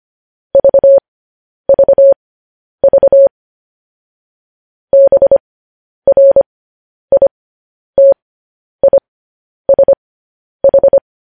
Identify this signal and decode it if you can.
Morse code, heard as a keyed tone, about 25 wpm, spaced as for 10 wpm, 550 Hz. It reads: VVV BRITISH
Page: News Headlines in Morse Code at 10 WPM